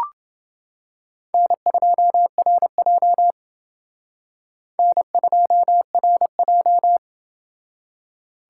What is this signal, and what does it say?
Morse code, heard as a keyed tone, about 29 wpm, 700 Hz. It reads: N2RJ N2RJ